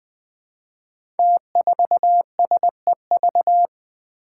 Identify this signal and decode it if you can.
Morse code, heard as a keyed tone, about 20 wpm, 700 Hz. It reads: T4SEV
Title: Five Challenging Letter-Number Combinations 20wpm